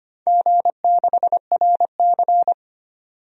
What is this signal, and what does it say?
Morse code, heard as a keyed tone, about 25 wpm, 700 Hz. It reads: G6RC
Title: DXCC-100 Call Signs - 4 Characters - 25wpm